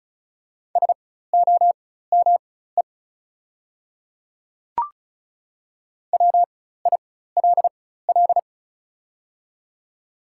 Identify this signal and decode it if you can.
Morse code, heard as a keyed tone, about 36 wpm, spaced as for 16 wpm, 690 Hz. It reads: SOME WILL
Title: 4x Character Spacing - Top 100 Words - 35wpm